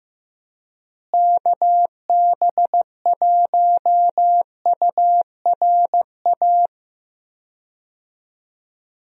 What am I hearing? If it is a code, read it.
Morse code, heard as a keyed tone, about 16 wpm, 700 Hz. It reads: KB1URA
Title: USA Call Signs - 6 Characters - Repeated 3x - 15wpm